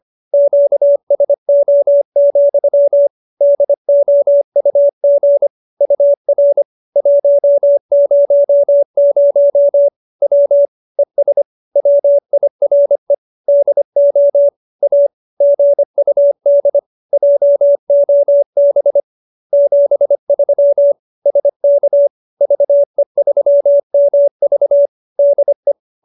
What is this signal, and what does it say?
Morse code, heard as a keyed tone, about 25 wpm, 570 Hz. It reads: QSO, DOUG UR 100 W ES WIRE DO A GUD JOB 73 SK VE3MV DE